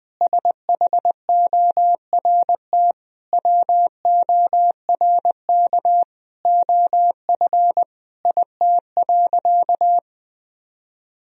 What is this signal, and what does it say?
Morse code, heard as a keyed tone, about 21 wpm, 700 Hz. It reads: SHORT WORK OF IT.
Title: Sentences from Top 1000 Words - Review Part 2 - 20wpm